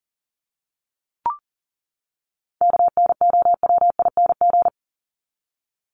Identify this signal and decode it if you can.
Morse code, heard as a keyed tone, about 40 wpm, 700 Hz. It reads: KNOWING
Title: ING Suffix - Top 100 Words with Base Words 40wpm